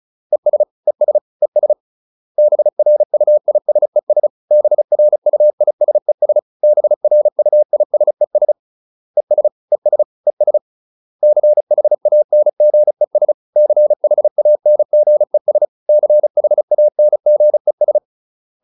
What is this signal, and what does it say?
Morse code, heard as a keyed tone, about 34 wpm, 600 Hz. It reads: ES ES ES BRUISES BRUISES BRUISES ES ES ES CHANGES CHANGES CHANGES